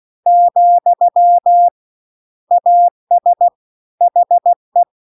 Morse code, heard as a keyed tone, about 16 wpm, 700 Hz. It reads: , AS HE